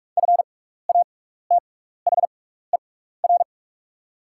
Morse code, heard as a keyed tone, about 44 wpm, spaced as for 15 wpm, 700 Hz. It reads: FATHER
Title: 6x Character Spacing - Top 300 Words - 45wpm